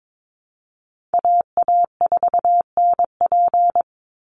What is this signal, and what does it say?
Morse code, heard as a keyed tone, about 22 wpm, 700 Hz. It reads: AA4NP
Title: Call Signs Repeated 2x - 22wpm